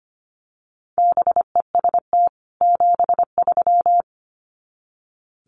Morse code, heard as a keyed tone, about 25 wpm, 700 Hz. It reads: BEST 73
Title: Common QSO Phrases 25wpm